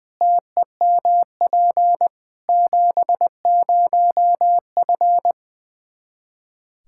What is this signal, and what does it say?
Morse code, heard as a keyed tone, about 21 wpm, 700 Hz. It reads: TEMP 70F
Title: K1USN SST QSO Practice 20wpm